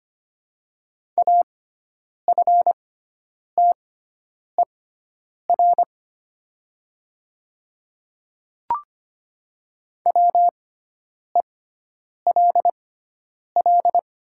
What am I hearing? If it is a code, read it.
Morse code, heard as a keyed tone, about 25 wpm, spaced as for 9 wpm, 700 Hz. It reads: AFTER WELL
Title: 6x Character Spacing - Top 200 Words - 25wpm